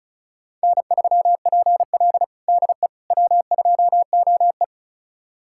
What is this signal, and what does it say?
Morse code, heard as a keyed tone, about 34 wpm, 700 Hz. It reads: N3PL DE W2OE
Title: USA Callsign DE Callsign - 4 Characters - 35wpm